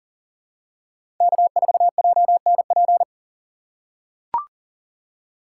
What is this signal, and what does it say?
Morse code, heard as a keyed tone, about 38 wpm, 700 Hz. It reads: K4JNP
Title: USA Call Signs - 5 Characters - 4x Character Spacing - 40wpm